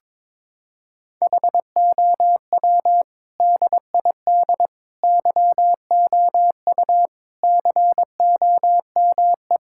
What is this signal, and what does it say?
Morse code, heard as a keyed tone, about 22 wpm, 700 Hz. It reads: HOW DID YOU COME
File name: Sentences with 7-Word Phrases - 22wpm